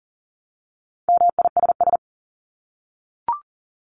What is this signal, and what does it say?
Morse code, heard as a keyed tone, about 40 wpm, 700 Hz. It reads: MISS
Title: Top 500 Words 40wpm